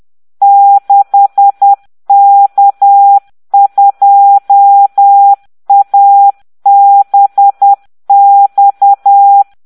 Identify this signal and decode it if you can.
Morse code, heard as a keyed tone, about 10 wpm, 800 Hz. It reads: 6K2ABX